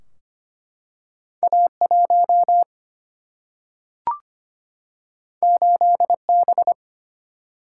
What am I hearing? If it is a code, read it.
Morse code, heard as a keyed tone, about 25 wpm, 700 Hz. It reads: A1 8B
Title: Single-Letter Single-Number 25wpm